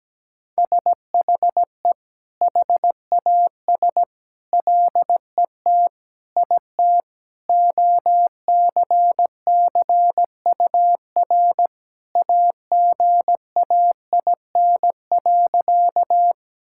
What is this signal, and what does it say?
Morse code, heard as a keyed tone, about 17 wpm, 700 Hz. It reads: SHE HAS LET IT OCCUR AGAIN.
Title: Sentences from Top 1000 Words - Review Part 6 - 17wpm